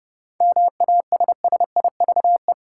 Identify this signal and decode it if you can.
Morse code, heard as a keyed tone, about 32 wpm, 700 Hz. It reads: MASSIVE
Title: Seven Letter Words - 30wpm